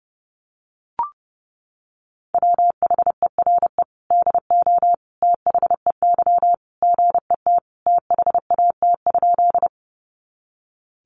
Morse code, heard as a keyed tone, about 30 wpm, 700 Hz. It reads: WHERE DO THEY GET THAT?
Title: Sentences from Top 200 Words 30wpm